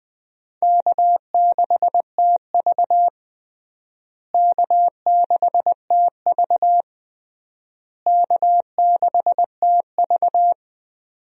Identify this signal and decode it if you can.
Morse code, heard as a keyed tone, about 20 wpm, 700 Hz. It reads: K6TV K6TV K6TV
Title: USA Call Signs - 4 Characters - Repeated 3x - 20wpm